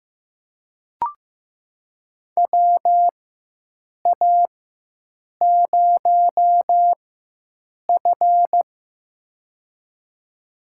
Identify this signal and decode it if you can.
Morse code, heard as a keyed tone, about 15 wpm, spaced as for 7 wpm, 700 Hz. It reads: WA0F
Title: USA Call Signs - 4 Characters - 4x Character Spacing - 15wpm